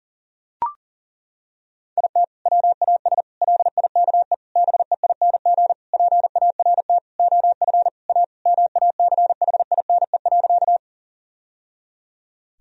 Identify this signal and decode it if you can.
Morse code, heard as a keyed tone, about 40 wpm, 700 Hz. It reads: IT WAS LIKE BEING PART OF A MACHINE.
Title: Sentences from Top 1000 Words - Review Part 3 - 40wpm